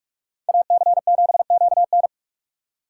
Morse code, heard as a keyed tone, about 43 wpm, 700 Hz. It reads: AC7QN